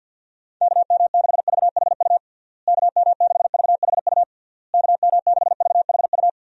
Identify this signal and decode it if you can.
Morse code, heard as a keyed tone, about 52 wpm, 700 Hz. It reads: KM6VHU KM6VHU KM6VHU
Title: USA Call Signs - 6 Characters - Repeated 3x - 50wpm